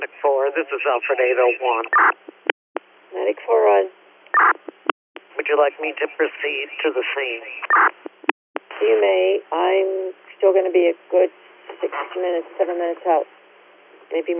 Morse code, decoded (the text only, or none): none